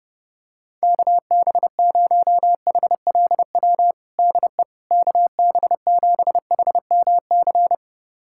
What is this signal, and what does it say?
Morse code, heard as a keyed tone, about 32 wpm, 700 Hz. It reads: KB0HLW DE KB7HMC
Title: USA Callsign DE Callsign - 6 Characters - 30wpm